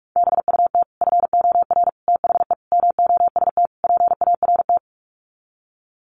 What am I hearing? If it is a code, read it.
Morse code, heard as a keyed tone, about 44 wpm, 700 Hz. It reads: BUT FOR THE MOST PART
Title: Sets of 5 Words Encore 45wpm